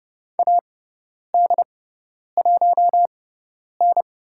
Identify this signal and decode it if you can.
Morse code, heard as a keyed tone, about 30 wpm, spaced as for 10 wpm, 700 Hz. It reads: AD1N
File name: Lesson 16g - Letter c - Call Signs 30wpm